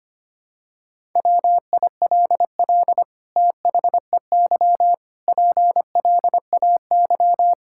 Morse code, heard as a keyed tone, about 25 wpm, 700 Hz. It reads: WILL THEY PLAY
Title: Sentences from Top 1000 Words - Review Part 6 - 25wpm